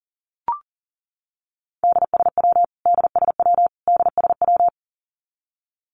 Morse code, heard as a keyed tone, about 40 wpm, 700 Hz. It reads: DSW DSW DSW
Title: Abbreviations Repeated 3x - 40wpm